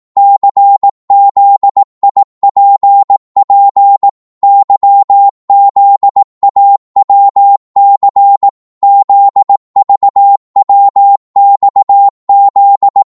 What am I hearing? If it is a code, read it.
Morse code, heard as a keyed tone, about 18 wpm, 800 Hz. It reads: CZIPP YZAWC ZVWXZ
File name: Morse / ltr-18wpm